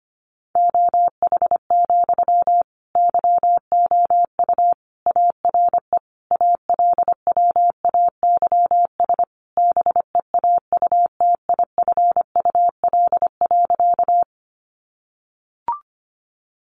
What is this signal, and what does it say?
Morse code, heard as a keyed tone, about 25 wpm, 700 Hz. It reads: OH, YOU ARE ALWAYS BEAUTIFUL.
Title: Sentences from Top 900 Words 25wpm